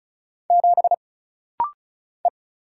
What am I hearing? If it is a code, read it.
Morse code, heard as a keyed tone, about 34 wpm, 700 Hz. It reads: 7 E
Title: Single Letter-Number Alt - Rapid-Fire 35wpm